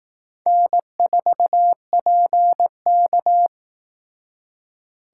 Morse code, heard as a keyed tone, about 18 wpm, 700 Hz. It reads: N4PK
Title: Speed Racing - USA 1x2 Call Signs 15wpm